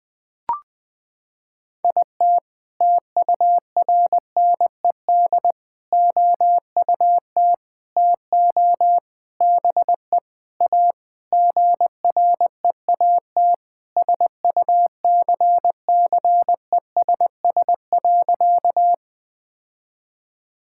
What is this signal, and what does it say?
Morse code, heard as a keyed tone, about 20 wpm, 700 Hz. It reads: IT TURNED OUT TO BE A GREAT SUCCESS.
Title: Sentences with 6-Word Phrases - 20wpm